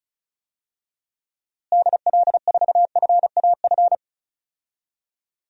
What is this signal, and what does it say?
Morse code, heard as a keyed tone, about 36 wpm, 700 Hz. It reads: DL4FAF
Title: DXCC-100 Call Signs - 6 Characters - Repeated 3x - 35wpm